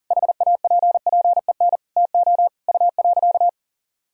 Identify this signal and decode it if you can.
Morse code, heard as a keyed tone, about 40 wpm, 700 Hz. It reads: HAPPEN TO U.
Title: Sentences from Top 500 Words and Abbreviations 40wpm